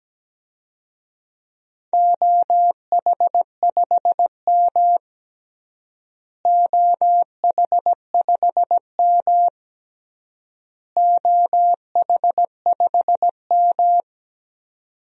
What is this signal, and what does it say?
Morse code, heard as a keyed tone, about 17 wpm, 700 Hz. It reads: OH5M OH5M OH5M
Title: DXCC-100 Call Signs - 4 Characters - Repeated 3x - 17wpm